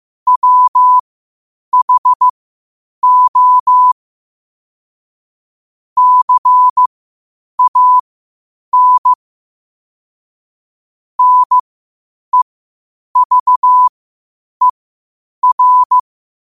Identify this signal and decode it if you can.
Morse code, heard as a keyed tone, about 15 wpm, spaced as for 8 wpm, 1000 Hz. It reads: WHO CAN NEVER
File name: qotdfiles / 8wpm